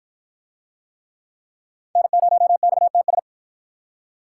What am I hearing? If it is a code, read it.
Morse code, heard as a keyed tone, about 48 wpm, 700 Hz. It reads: N0XTS